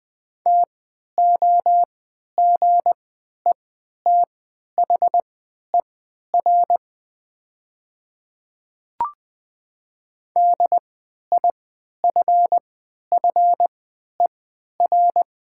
Morse code, heard as a keyed tone, about 20 wpm, spaced as for 11 wpm, 700 Hz. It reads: TOGETHER DIFFER
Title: 3x Character Spacing - Top 300 Words - 20wpm